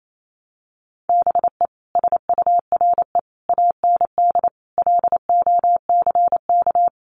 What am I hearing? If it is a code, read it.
Morse code, heard as a keyed tone, about 28 wpm, 700 Hz. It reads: BE SURE AND LOCK
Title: Sentences from Top 1300 Words 28wpm